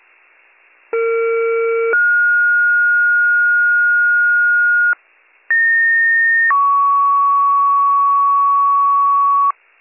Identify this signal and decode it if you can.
Two-tone page, heard as a one-second tone, then a three-second tone, about 460 then 1430 Hz, about 1850 then 1120 Hz, with noise. It is heard from 0.9 to 9.5 s.